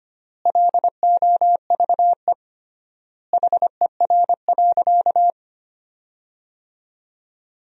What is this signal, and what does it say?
Morse code, heard as a keyed tone, about 25 wpm, 700 Hz. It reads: LOVE HER.
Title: 3x Word-Spacing - Sentences from Top 400 Words - 25wpm